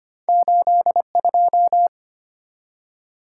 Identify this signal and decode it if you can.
Morse code, heard as a keyed tone, about 25 wpm, 700 Hz. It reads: 82
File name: Two Letter-Number Combinations - 25wpm